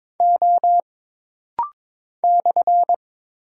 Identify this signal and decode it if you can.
Morse code, heard as a keyed tone, about 22 wpm, 700 Hz. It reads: O /
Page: This rapid-fire timing is geared to push your ICR (Instant Character Recognition) skills.